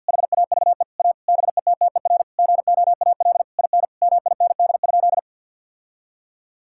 Morse code, heard as a keyed tone, about 50 wpm, 690 Hz. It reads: HAVE A BETTER GOAL IN MIND?
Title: Sentences from Top 1000 Words - Review Part 5 - 50wpm